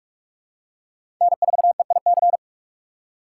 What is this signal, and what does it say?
Morse code, heard as a keyed tone, about 47 wpm, 690 Hz. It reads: N4EIC
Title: USA Call Signs - 5 Characters - 6x Character Spacing - 45wpm